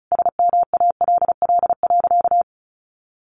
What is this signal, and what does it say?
Morse code, heard as a keyed tone, about 34 wpm, 700 Hz. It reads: SMALL.
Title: Sentences from Top 1400 Words 35wpm